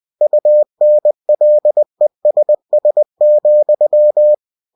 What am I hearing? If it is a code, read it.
Morse code, heard as a keyed tone, about 20 wpm, 600 Hz. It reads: UNLESS,
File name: cw-books / winnie-the-pooh / winnie-20wpm-page-0037